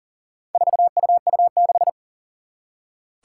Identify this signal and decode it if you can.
Morse code, heard as a keyed tone, about 40 wpm, 700 Hz. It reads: 4UU6
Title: Four Challenging Letter-Number Combinations 40wpm